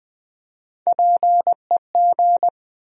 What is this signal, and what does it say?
Morse code, heard as a keyed tone, about 20 wpm, 700 Hz. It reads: PEG
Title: USA Common Names - Max 3 Letters - 20wpm